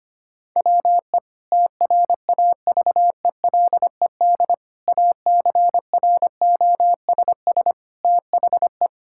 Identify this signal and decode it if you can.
Morse code, heard as a keyed tone, about 25 wpm, 700 Hz. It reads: WE TRAVELED ACROSS THE
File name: practice / 25wpm / Sentences with 2-Word Phrases - 25wpm